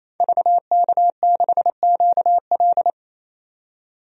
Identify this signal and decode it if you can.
Morse code, heard as a keyed tone, about 28 wpm, 700 Hz. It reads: VK6QL
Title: DXCC-100 Call Signs - 5 Characters - 28wpm